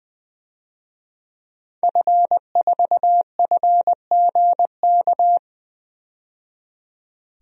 Morse code, heard as a keyed tone, about 21 wpm, 700 Hz. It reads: F4FGK
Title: DXCC-100 Call Signs - 5 Characters - 20wpm